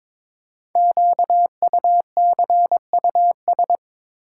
Morse code, heard as a keyed tone, about 23 wpm, 700 Hz. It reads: QUCUS
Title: Code Groups - 5 Letters - 22wpm